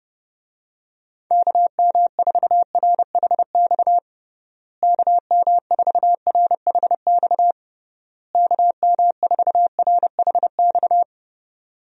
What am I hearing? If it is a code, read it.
Morse code, heard as a keyed tone, about 32 wpm, 700 Hz. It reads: KM4RHX KM4RHX KM4RHX
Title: USA Call Signs - 6 Characters - Repeated 3x - 30wpm